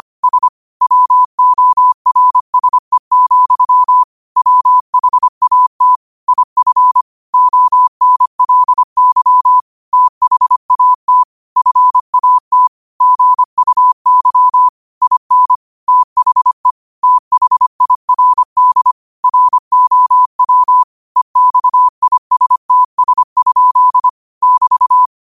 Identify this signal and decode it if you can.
Morse code, heard as a keyed tone, about 25 wpm, 1000 Hz. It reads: S WORSE, WHAT IF ONLY THAT FAT GUY IN THE THIRD ROW EXISTS? <BT>
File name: qotdfiles / 25wpm